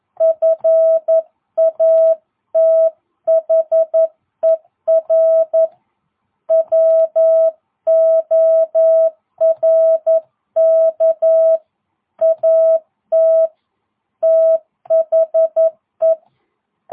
Morse code, heard as a keyed tone, about 11 wpm, 660 Hz. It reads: FATHER WORK AT THE